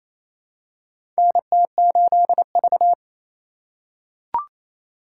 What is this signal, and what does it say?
Morse code, heard as a keyed tone, about 28 wpm, 700 Hz. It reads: NT8V
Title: USA Call Signs - 3x Character Spacing - 28wpm